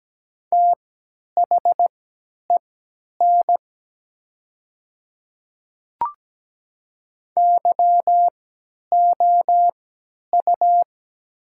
Morse code, heard as a keyed tone, about 17 wpm, spaced as for 10 wpm, 700 Hz. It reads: THEN YOU
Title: 3x Character Spacing - Top 100 Words - 17wpm